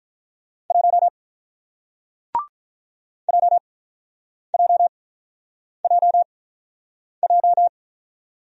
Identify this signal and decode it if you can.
Morse code, heard as a keyed tone, about 48 wpm, spaced as for 9 wpm, 700 Hz. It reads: 1 JJJJ